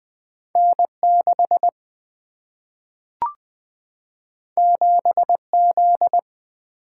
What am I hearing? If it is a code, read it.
Morse code, heard as a keyed tone, about 20 wpm, 700 Hz. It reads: N6 7Z
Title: Single-Letter Single-Number 20wpm